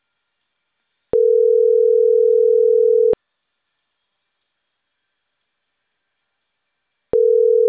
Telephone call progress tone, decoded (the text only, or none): ringback tone